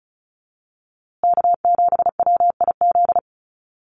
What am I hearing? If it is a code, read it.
Morse code, heard as a keyed tone, about 34 wpm, 700 Hz. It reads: K7WIZ
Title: USA Call Signs Repeated 2x 35wpm